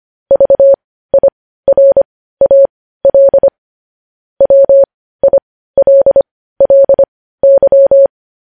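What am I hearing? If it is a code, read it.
Morse code, heard as a keyed tone, about 25 wpm, spaced as for 15 wpm, 550 Hz. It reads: VIRAL WILLY
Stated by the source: News headlines sent in Morse code at 15 words per minute, preceded by vvv.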